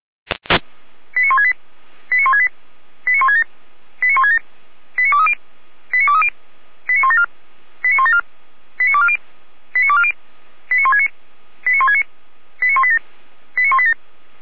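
Five-tone selective call (ZVEI), heard as quick runs of five tones, tones of about 80 ms. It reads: 89168 89168 89167 89167 89230 89230 89174 89174 89240 89240 89169 89169 89187 89187
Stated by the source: SDR-received ZVEI five-tone call 89130.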